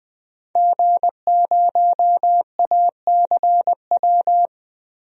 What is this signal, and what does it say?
Morse code, heard as a keyed tone, about 21 wpm, 700 Hz. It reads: G0ACW
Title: DXCC-100 Call Signs - 5 Characters - Repeated 3x - 20wpm